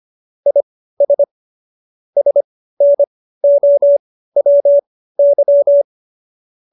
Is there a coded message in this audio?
Yes: Morse code from 0.5 to 5.8 s, heard as a keyed tone, about 25 wpm, spaced as for 15 wpm, 570 Hz. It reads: IS SNOWY